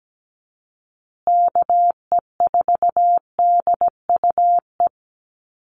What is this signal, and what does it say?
Morse code, heard as a keyed tone, about 17 wpm, 700 Hz. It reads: KE4DUE
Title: USA Call Signs Repeated 2x 17wpm